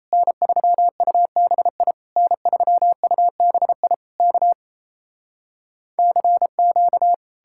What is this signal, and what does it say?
Morse code, heard as a keyed tone, about 37 wpm, 700 Hz. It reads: N3UBI N3UBI K CQ